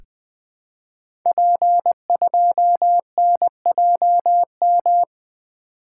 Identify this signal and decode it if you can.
Morse code, heard as a keyed tone, about 20 wpm, 700 Hz. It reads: P2NJM
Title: Code Groups - 5 Letter:Numbers - 20wpm